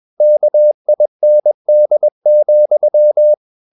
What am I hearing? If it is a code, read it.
Morse code, heard as a keyed tone, about 21 wpm, 600 Hz. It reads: KIND,
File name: marienkind2 21wpm